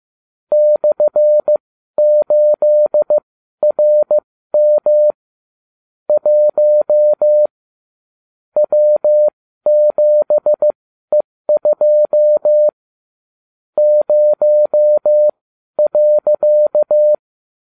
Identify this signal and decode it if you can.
Morse code, heard as a keyed tone, about 15 wpm, spaced as for 12 wpm, 600 Hz. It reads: /8RM 1 W7E2 0.